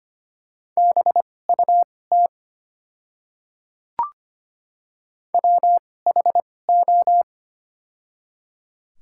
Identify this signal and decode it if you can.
Morse code, heard as a keyed tone, about 25 wpm, spaced as for 18 wpm, 700 Hz. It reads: BUT WHO
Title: 2x Character Spacing - Top 300 Words - 25wpm